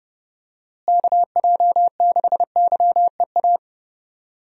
Morse code, heard as a keyed tone, about 30 wpm, 700 Hz. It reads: KJ6YEA
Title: USA Call Signs - 6 Characters - Repeated 2x - 30wpm